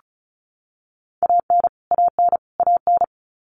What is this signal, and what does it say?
Morse code, heard as a keyed tone, about 34 wpm, 700 Hz. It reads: AN AN AN
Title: Top 200 Words Repeated 35wpm